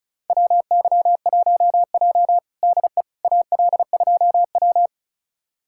Morse code, heard as a keyed tone, about 36 wpm, 700 Hz. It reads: WY1J DE AL2W
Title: USA Callsign DE Callsign - 4 Characters - 35wpm